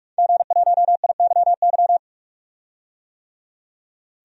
Morse code, heard as a keyed tone, about 46 wpm, 700 Hz. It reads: G1IYY
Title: DXCC-100 Call Signs - 5 Characters - 45wpm